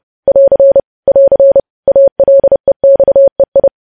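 Morse code, heard as a keyed tone, about 30 wpm, 550 Hz. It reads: <AR> <AR> ALEXEI
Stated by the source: Morse practice of news headlines at 30wpm.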